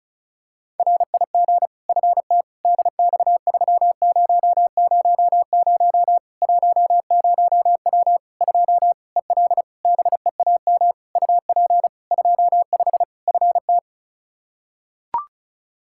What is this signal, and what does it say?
Morse code, heard as a keyed tone, about 36 wpm, 700 Hz. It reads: RIG FT DX3000 10W 2 EL BEAM UP 25 FT
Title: Practice QSO - Exchange 2 - Info - 35wpm